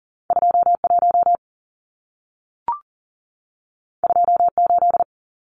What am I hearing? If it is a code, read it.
Morse code, heard as a keyed tone, about 40 wpm, 700 Hz. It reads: 21 28
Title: Numbers - 2-Digit 40wpm